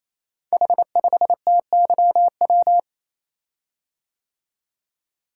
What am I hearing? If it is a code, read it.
Morse code, heard as a keyed tone, about 27 wpm, 700 Hz. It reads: H5TYW
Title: Code Groups - 5 Letter:Numbers - 28wpm